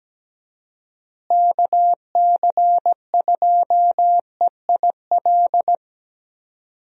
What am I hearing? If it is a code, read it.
Morse code, heard as a keyed tone, about 17 wpm, 700 Hz. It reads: KC2EIL
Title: USA Call Signs - 6 Characters - Repeated 2x - 17wpm